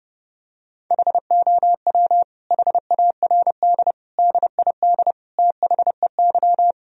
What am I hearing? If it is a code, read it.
Morse code, heard as a keyed tone, about 32 wpm, 700 Hz. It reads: HOW HARD DID THEY
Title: Sentences from Top 1000 Words - Review Part 4 - 30wpm